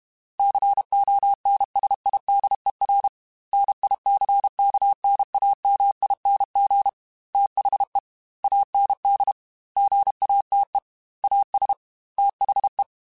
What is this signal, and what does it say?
Morse code, heard as a keyed tone, about 32 wpm, 800 Hz. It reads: CONSIDER NICKNAMING THE AND GATE AS THE
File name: Technical Subjects / Technical Subjects-mp3s / AND Gate @30